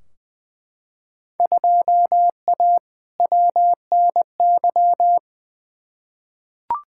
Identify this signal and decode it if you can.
Morse code, heard as a keyed tone, about 20 wpm, 700 Hz. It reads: 2A WNY
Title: Field Day Exchanges - 20wpm